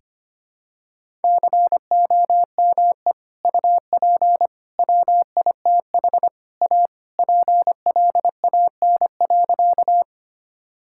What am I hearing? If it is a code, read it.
Morse code, heard as a keyed tone, about 25 wpm, 700 Hz. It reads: COME UP WITH A PLAN.
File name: practice / 25wpm / Sentences with 3-Word Phrases - 25wpm